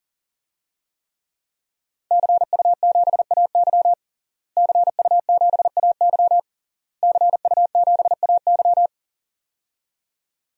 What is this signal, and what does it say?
Morse code, heard as a keyed tone, about 40 wpm, 700 Hz. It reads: CU7AY CU7AY CU7AY
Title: DXCC-100 Call Signs - 5 Characters - Repeated 3x - 40wpm